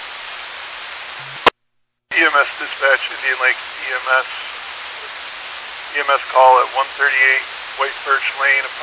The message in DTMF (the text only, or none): none